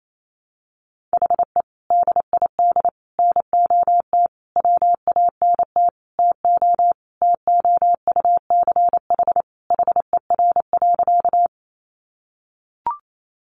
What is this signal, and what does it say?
Morse code, heard as a keyed tone, about 28 wpm, 700 Hz. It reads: HE DID NOT WANT TO TOUCH HER.
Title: Sentences from Top 1200 Words 28wpm